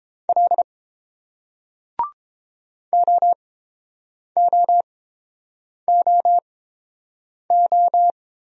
Morse code, heard as a keyed tone, about 30 wpm, spaced as for 7 wpm, 700 Hz. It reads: L OOOO